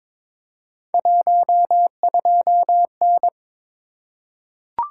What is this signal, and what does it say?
Morse code, heard as a keyed tone, about 22 wpm, 700 Hz. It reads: 12N
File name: practice / 22wpm / Three Letter-Number Combinations - 22wpm